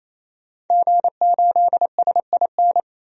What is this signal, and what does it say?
Morse code, heard as a keyed tone, about 28 wpm, 700 Hz. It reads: G8SIN
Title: DXCC-100 Call Signs - 5 Characters - 28wpm